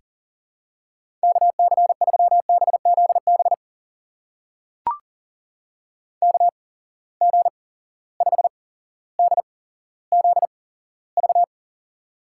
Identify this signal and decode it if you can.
Morse code, heard as a keyed tone, about 40 wpm, spaced as for 17 wpm, 700 Hz. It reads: KC3BZB KG5DZV
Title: USA Call Signs - 6 Characters - 8x Character Spacing - 40wpm